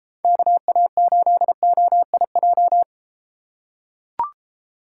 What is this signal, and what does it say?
Morse code, heard as a keyed tone, about 33 wpm, 700 Hz. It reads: KA8OIJ